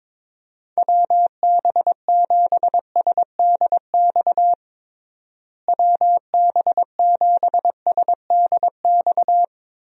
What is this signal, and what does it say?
Morse code, heard as a keyed tone, about 22 wpm, 700 Hz. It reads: WB7SDX WB7SDX
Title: USA Call Signs - 6 Characters - Repeated 3x - 22wpm